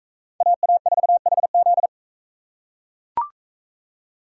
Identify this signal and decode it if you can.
Morse code, heard as a keyed tone, about 40 wpm, 700 Hz. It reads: AA4HZ